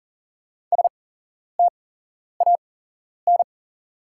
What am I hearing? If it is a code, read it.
Morse code, heard as a keyed tone, about 41 wpm, spaced as for 11 wpm, 700 Hz. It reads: STAN
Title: 8x Character Spacing - Top 300 Words - 40wpm